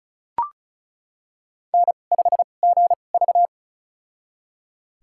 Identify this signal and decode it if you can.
Morse code, heard as a keyed tone, about 36 wpm, spaced as for 25 wpm, 700 Hz. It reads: N5GV
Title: USA Call Signs - 4 Characters - 2x Character Spacing - 35wpm